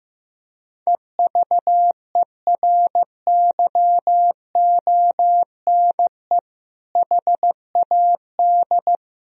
Morse code, heard as a keyed tone, about 15 wpm, 700 Hz. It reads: EVERYONE HAD
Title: Sentences from Top 1000 Words - Review Part 2 - 15wpm